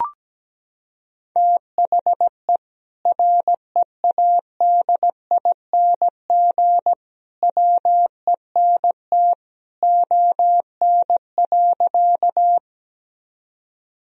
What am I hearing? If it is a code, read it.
Morse code, heard as a keyed tone, about 17 wpm, 700 Hz. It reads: THE READING WENT ON.
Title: Sentences from Top 1000 Words - Review Part 3 - 17wpm